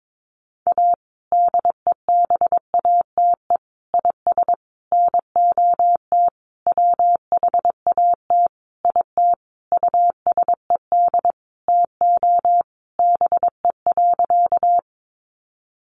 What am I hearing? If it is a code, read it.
Morse code, heard as a keyed tone, about 22 wpm, 700 Hz. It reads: A DEBATE IS NOT WHAT IT USED TO BE.
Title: Sentences from Top 1100 Words 22wpm